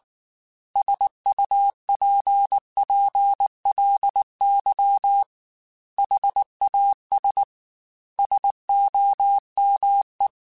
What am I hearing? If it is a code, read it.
Morse code, heard as a keyed tone, about 19 wpm, 800 Hz. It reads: SUPPLY HAS SOME
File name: Technical Subjects / Technical Subjects-mp3s / Sw Pwr Sup @18